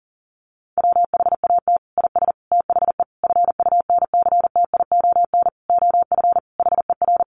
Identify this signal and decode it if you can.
Morse code, heard as a keyed tone, about 40 wpm, 700 Hz. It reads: WHAT IS THE FUNCTION OF HER
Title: Sentences from Top 1100 Words 40wpm